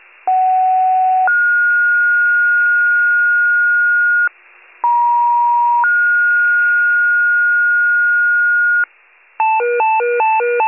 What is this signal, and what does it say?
Two-tone page, heard as a one-second tone, then a three-second tone, about 750 then 1430 Hz, about 950 then 1430 Hz. It is heard from 0.3 to 8.8 s.
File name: wcalert